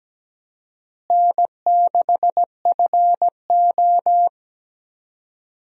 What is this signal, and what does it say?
Morse code, heard as a keyed tone, about 17 wpm, 700 Hz. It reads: N6FO
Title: USA Call Signs - 4 Characters - Repeated 2x - 17wpm